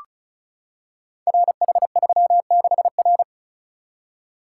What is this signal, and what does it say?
Morse code, heard as a keyed tone, about 36 wpm, 700 Hz. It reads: RH36R
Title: Code Groups - 5 Letter:Numbers - 35wpm